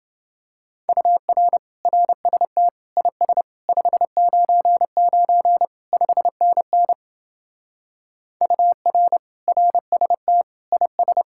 Morse code, heard as a keyed tone, about 29 wpm, 700 Hz. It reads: UR RST IS 599 5NN UR RST IS